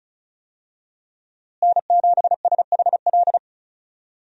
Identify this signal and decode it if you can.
Morse code, heard as a keyed tone, about 36 wpm, 700 Hz. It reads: N7SHL
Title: USA Call Signs - 5 Characters - Repeated 2x - 35wpm